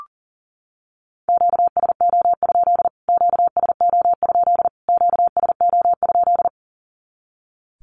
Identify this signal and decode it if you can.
Morse code, heard as a keyed tone, about 40 wpm, 700 Hz. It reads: QSO? QSO? QSO?
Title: Q-Codes Repeated 3x 40wpm